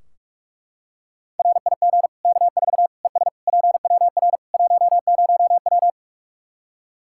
Morse code, heard as a keyed tone, about 41 wpm, 700 Hz. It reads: RIG K4 ES PWR 10W
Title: Practice QSO - Exchange 2 - Info - 45wpm